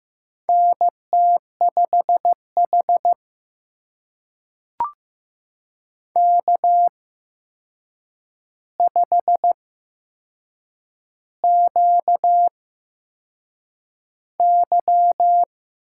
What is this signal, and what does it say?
Morse code, heard as a keyed tone, about 15 wpm, spaced as for 6 wpm, 700 Hz. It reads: NT5H K5QY
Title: USA Call Signs - 8x Character Spacing - 15wpm